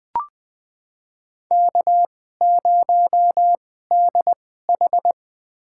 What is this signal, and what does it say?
Morse code, heard as a keyed tone, about 20 wpm, spaced as for 15 wpm, 700 Hz. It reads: K0DH
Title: USA Call Signs - 4 Characters - 2x Character Spacing - 20wpm